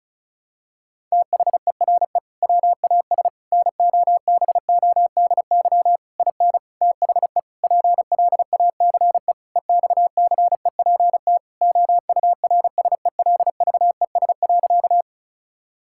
Morse code, heard as a keyed tone, about 36 wpm, 700 Hz. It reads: THERE WAS NOBODY IN THE PLACE EXCEPT OURSELVES.
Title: Sentences from Top 1600 Words 35wpm